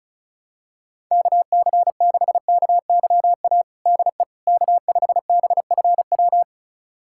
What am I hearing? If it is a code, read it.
Morse code, heard as a keyed tone, about 34 wpm, 690 Hz. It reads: KC6KYA DE K5BFW
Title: USA Callsign DE Callsign 35wpm